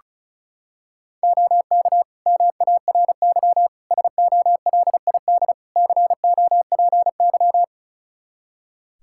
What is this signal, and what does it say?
Morse code, heard as a keyed tone, about 34 wpm, 700 Hz. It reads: OK MARY SOLID COPY
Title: Practice QSO - Exchange 3 - More Info - 35wpm